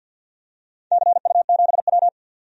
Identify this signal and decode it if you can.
Morse code, heard as a keyed tone, about 46 wpm, 700 Hz. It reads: CU7W
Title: DXCC-100 Call Signs - 4 Characters - 50wpm